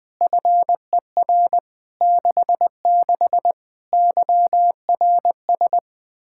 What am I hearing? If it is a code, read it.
Morse code, heard as a keyed tone, about 21 wpm, 700 Hz. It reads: FER 66 YRS
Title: Practice QSO - Exchange 3 - More Info - 20wpm